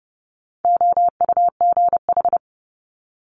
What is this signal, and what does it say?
Morse code, heard as a keyed tone, about 31 wpm, 700 Hz. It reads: OUGH
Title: Multi-letter Phonemes Quick Fire 30wpm